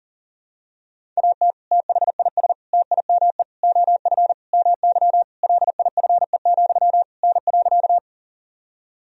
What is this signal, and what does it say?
Morse code, heard as a keyed tone, about 40 wpm, 700 Hz. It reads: AT THIS TIME OF MY LIFE, N.
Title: Sentences from Top 300 Words and Abbreviations 40wpm